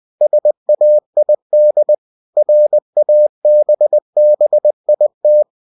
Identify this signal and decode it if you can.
Morse code, heard as a keyed tone, about 20 wpm, 600 Hz. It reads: SAID RABBIT
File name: cw-books / winnie-the-pooh / winnie-20wpm-page-0019